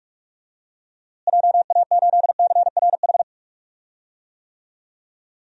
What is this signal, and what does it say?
Morse code, heard as a keyed tone, about 46 wpm, 700 Hz. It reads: JA8CRH
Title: DXCC-100 Call Signs - 6 Characters - Repeated 3x - 45wpm